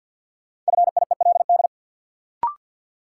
Morse code, heard as a keyed tone, about 53 wpm, 700 Hz. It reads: FIELD
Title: Sentences from Top 1000 Words - New Word Drill - 50wpm